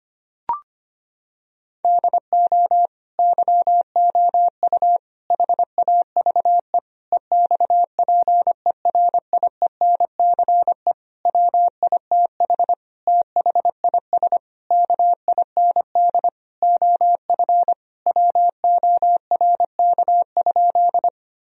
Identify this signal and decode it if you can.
Morse code, heard as a keyed tone, about 25 wpm, 700 Hz. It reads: DO YOU HAVE EXPERIENCE WITH THIS KIND OF WORK?
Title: Sentences from Top 1000 Words - Review Part 3 - 25wpm